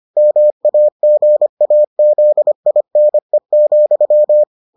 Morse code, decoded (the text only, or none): MAGAZINE,